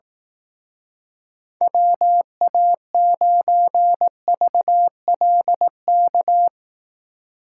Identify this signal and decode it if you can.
Morse code, heard as a keyed tone, about 17 wpm, 700 Hz. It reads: WA9VLK
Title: Speed Racing - USA 2x3 Call Signs 15wpm